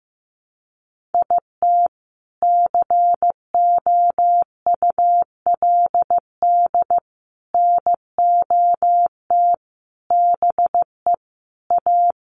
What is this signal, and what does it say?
Morse code, heard as a keyed tone, about 15 wpm, 700 Hz. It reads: IT COULD NOT BE A
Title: Sentences from Top 1500 Words 15wpm